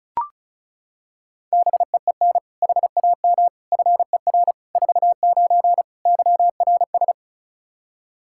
Morse code, heard as a keyed tone, about 36 wpm, 700 Hz. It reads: BEEN HAM FER 49 YRS
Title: Practice QSO - Exchange 3 - More Info - 35wpm